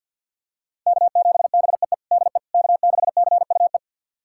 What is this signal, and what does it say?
Morse code, heard as a keyed tone, about 53 wpm, 700 Hz. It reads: K7BEE DE K6CUE
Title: USA Callsign DE Callsign - 5 Characters - 50wpm